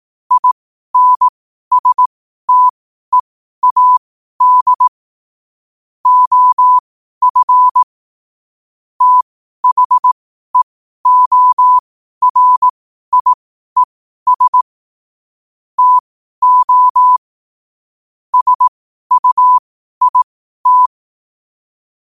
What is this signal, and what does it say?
Morse code, heard as a keyed tone, about 18 wpm, spaced as for 12 wpm, 1000 Hz. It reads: INSTEAD OF THEORIES TO SUIT
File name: qotdfiles / 12wpm